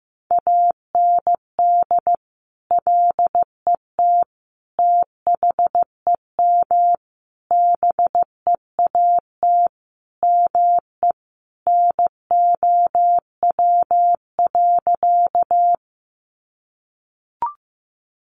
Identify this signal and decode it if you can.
Morse code, heard as a keyed tone, about 15 wpm, 700 Hz. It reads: AND LET THEM BEAT ME NOW.
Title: Sentences from Top 1300 Words 15wpm